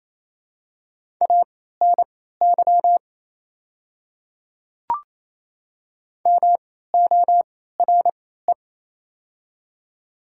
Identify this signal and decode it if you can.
Morse code, heard as a keyed tone, about 28 wpm, spaced as for 16 wpm, 700 Hz. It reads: ANY MORE
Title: 3x Character Spacing - Top 300 Words - 28wpm